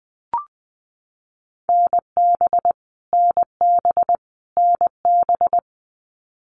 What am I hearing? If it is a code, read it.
Morse code, heard as a keyed tone, about 20 wpm, 700 Hz. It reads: NB NB NB
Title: Canadian Provinces and Territories - Repeated 3x - 20wpm